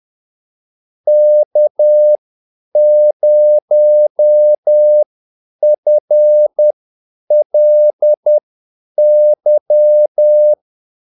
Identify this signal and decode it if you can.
Morse code, heard as a keyed tone, about 10 wpm, spaced as for 8 wpm, 600 Hz. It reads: K0FLY
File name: Callsigns-10-8wpm